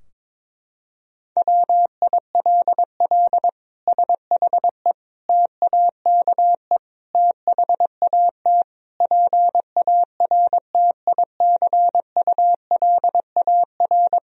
Morse code, heard as a keyed tone, about 22 wpm, 700 Hz. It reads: WILL SHE TAKE THAT PARTICULAR
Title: Sentences from Top 1000 Words - Review Part 4 - 22wpm